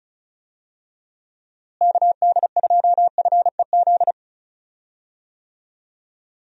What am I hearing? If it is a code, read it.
Morse code, heard as a keyed tone, about 36 wpm, 700 Hz. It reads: KD2FEZ
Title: USA Call Signs - 6 Characters - Repeated 2x - 35wpm